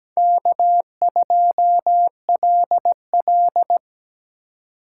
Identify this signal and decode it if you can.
Morse code, heard as a keyed tone, about 20 wpm, 700 Hz. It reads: K2LL
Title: Speed Racing - USA 1x2 Call Signs 17wpm